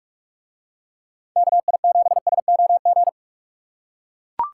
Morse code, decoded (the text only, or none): KI7SOG